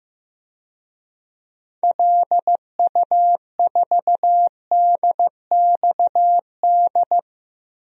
Morse code, heard as a keyed tone, about 15 wpm, 700 Hz. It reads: LU4DXD
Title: DXCC-100 Call Signs - 6 Characters - Repeated 3x - 15wpm